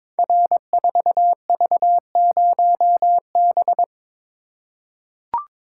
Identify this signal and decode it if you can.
Morse code, heard as a keyed tone, about 22 wpm, 700 Hz. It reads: R4V0B
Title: Code Groups - 5 Letter:Numbers - 22wpm